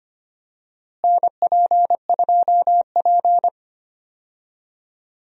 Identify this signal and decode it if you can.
Morse code, heard as a keyed tone, about 25 wpm, 700 Hz. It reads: NP2P
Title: USA Call Signs - 4 Characters - Repeated 3x - 25wpm